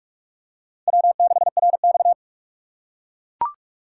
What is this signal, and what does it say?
Morse code, heard as a keyed tone, about 42 wpm, 700 Hz. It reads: W6RX
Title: USA Call Signs - 4 Characters - 6x Character Spacing - 45wpm